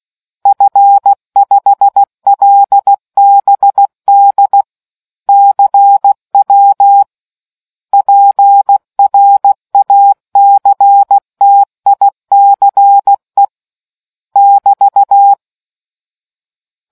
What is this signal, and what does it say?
Morse code, heard as a keyed tone, about 16 wpm, 800 Hz. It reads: F5LBD CW PRACTICE <BT>